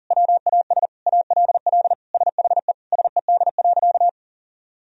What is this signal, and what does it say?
Morse code, heard as a keyed tone, about 40 wpm, 700 Hz. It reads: WAS ALL SHE SED.
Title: Sentences from Top 100 Words and Abbreviations 40wpm